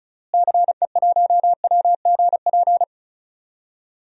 Morse code, heard as a keyed tone, about 36 wpm, 700 Hz. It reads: CE1WGP
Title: DXCC-100 Call Signs - 6 Characters - Repeated 3x - 35wpm